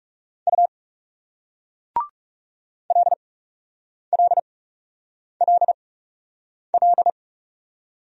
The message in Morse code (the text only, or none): U L L L L